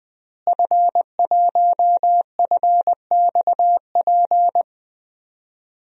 Morse code, heard as a keyed tone, about 20 wpm, 700 Hz. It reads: F1FXP
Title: DXCC-100 Call Signs - 5 Characters - Repeated 3x - 20wpm